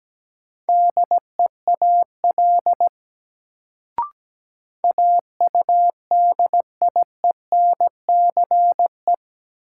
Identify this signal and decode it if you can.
Morse code, heard as a keyed tone, about 17 wpm, 700 Hz. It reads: DEAL AUDIENCE
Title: Sentences from Top 800 Words - New Word Drill - 17wpm